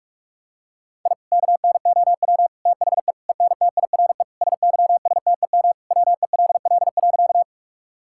Morse code, heard as a keyed tone, about 46 wpm, 700 Hz. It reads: I KNOW THE ENTIRE SYSTEM WELL.
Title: Sentences from Top 1000 Words - Review Part 5 - 45wpm